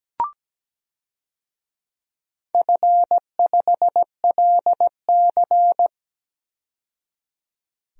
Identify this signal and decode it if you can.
Morse code, heard as a keyed tone, about 17 wpm, 700 Hz. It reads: F5LC
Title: DXCC-100 Call Signs - 4 Characters - 17wpm